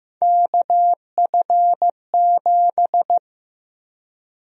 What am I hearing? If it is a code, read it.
Morse code, heard as a keyed tone, about 15 wpm, 700 Hz. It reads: KF7
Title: USA Call Sign Prefixes - Repeated 2x - 15wpm